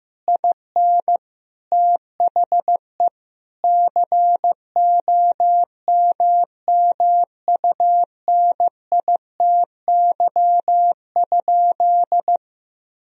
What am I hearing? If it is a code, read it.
Morse code, heard as a keyed tone, about 16 wpm, 700 Hz. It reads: IN THE COMMUNITY?
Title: Sentences from Top 1000 Words - Review Part 4 - 15wpm